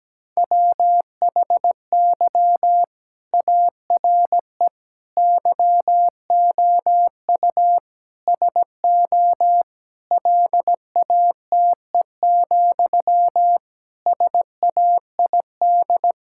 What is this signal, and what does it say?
Morse code, heard as a keyed tone, about 17 wpm, 700 Hz. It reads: WHY ARE YOU SO LATE, SAID
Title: Sentences from Top 1000 Words - Review Part 6 - 17wpm